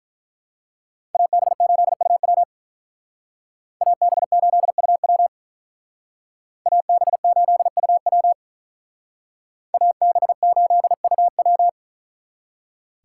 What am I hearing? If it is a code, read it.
Morse code, heard as a keyed tone, about 43 wpm, 700 Hz. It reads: AB8UW AB8UW AB8UW AB8UW
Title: Speed Racing - USA 2x2 Call Signs 35wpm